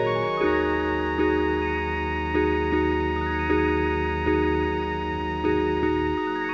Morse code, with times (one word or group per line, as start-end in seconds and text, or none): none